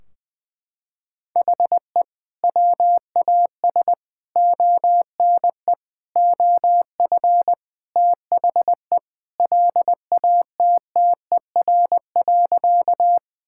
Morse code, heard as a keyed tone, about 21 wpm, 700 Hz. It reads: HE WAS ONE OF THE LATTER.
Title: Sentences from Top 1600 Words 20wpm